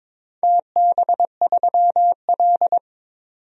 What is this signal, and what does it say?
Morse code, heard as a keyed tone, about 22 wpm, 700 Hz. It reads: TB3L
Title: DXCC-100 Call Signs - 4 Characters - 22wpm